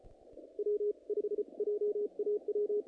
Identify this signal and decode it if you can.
Morse code, heard as a keyed tone, about 34 wpm, spaced as for 29 wpm, 420 Hz. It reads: W5JAW
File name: contests / naqp-cw-k5tr-op-k5pi- -1900